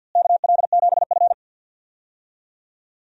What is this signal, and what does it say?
Morse code, heard as a keyed tone, about 52 wpm, 700 Hz. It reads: KL7F
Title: DXCC-100 Call Signs - 4 Characters - Repeated 3x - 50wpm